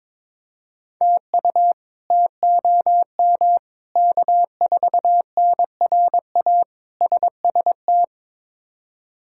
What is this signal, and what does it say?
Morse code, heard as a keyed tone, about 21 wpm, 700 Hz. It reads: TU TOM K4NRA SST
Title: K1USN SST QSO Practice - 22wpm